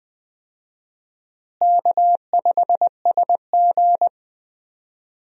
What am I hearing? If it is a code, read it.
Morse code, heard as a keyed tone, about 20 wpm, 700 Hz. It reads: K5SG
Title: USA Call Signs - 4 Characters - Repeated 3x - 20wpm